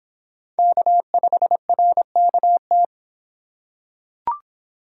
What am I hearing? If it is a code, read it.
Morse code, heard as a keyed tone, about 24 wpm, 700 Hz. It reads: K5RKT